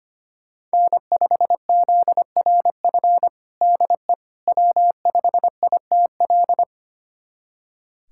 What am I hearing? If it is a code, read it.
Morse code, heard as a keyed tone, about 25 wpm, 700 Hz. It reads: N5ZRF DE W5ITL
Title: USA Callsign DE Callsign - 5 Characters - 25wpm